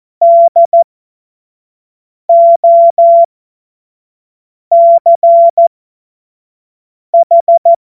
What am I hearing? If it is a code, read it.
Morse code, heard as a keyed tone, about 14 wpm, spaced as for 5 wpm, 680 Hz. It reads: DOCH
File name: Morsekurs / Witze / Witz 27. Juni 2023 25 F70